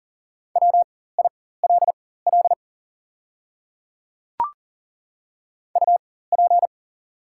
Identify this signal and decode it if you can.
Morse code, heard as a keyed tone, about 40 wpm, spaced as for 19 wpm, 690 Hz. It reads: WILL UP
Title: 4x Character Spacing - Top 200 Words - 40wpm